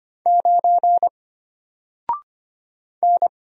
Morse code, heard as a keyed tone, about 25 wpm, 700 Hz. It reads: 9 N
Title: Single Letter-Number Alt 25wpm